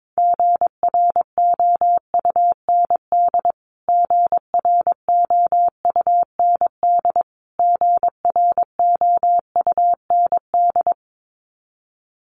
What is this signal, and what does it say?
Morse code, heard as a keyed tone, about 22 wpm, 700 Hz. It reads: GROUND GROUND GROUND
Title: Top 500 Words Repeated 22wpm